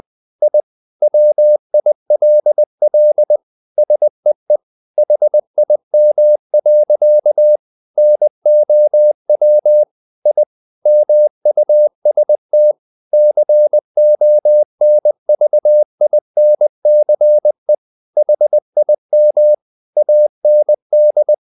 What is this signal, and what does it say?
Morse code, heard as a keyed tone, about 20 wpm, 600 Hz. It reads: I WILL SEE HIM. NOW I MUST CONVINCE HIM AND
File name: the old man and the sea part 2 20wpm